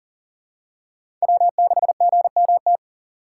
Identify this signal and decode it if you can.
Morse code, heard as a keyed tone, about 39 wpm, 700 Hz. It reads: W6GMT
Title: USA Call Signs - 5 Characters - 2x Character Spacing - 40wpm